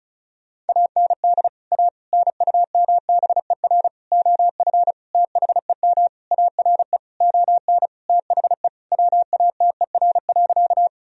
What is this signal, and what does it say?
Morse code, heard as a keyed tone, about 34 wpm, 700 Hz. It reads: AND A NUMBER OF THEM ARE ON THE WATER.
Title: Sentences from Top 1000 Words - Review Part 1 - 35wpm